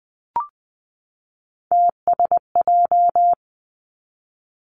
Morse code, heard as a keyed tone, about 20 wpm, 700 Hz. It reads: TSJ
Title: Three Challenging Letter-Number Combinations 20wpm